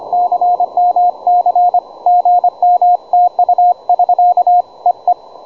Morse code, heard as a keyed tone, about 25 wpm, 710 Hz, with radio noise. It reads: CMC GM TU <SK> E E
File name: juma-trx2 / juma-trx2-17m-cw-sample